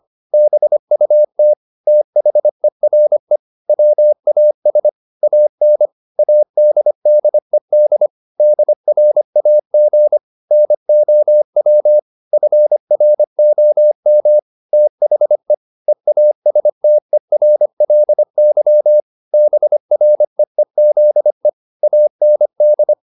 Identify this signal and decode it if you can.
Morse code, heard as a keyed tone, about 25 wpm, 600 Hz. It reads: BUT THERE WAS AN ADDED DRAG NOW FROM THE EASTERLY BREEZE AND
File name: the old man and the sea part 1 25wpm